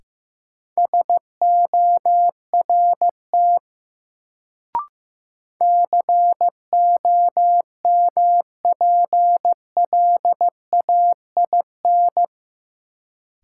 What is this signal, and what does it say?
Morse code, heard as a keyed tone, about 15 wpm, 700 Hz. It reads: SORT COMPLAIN
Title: Sentences from Top 1300 Words - New Word Drill Alt - 15wpm